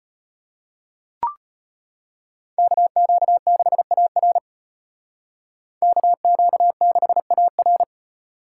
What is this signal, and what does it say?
Morse code, heard as a keyed tone, about 35 wpm, 700 Hz. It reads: KQ6AR KQ6AR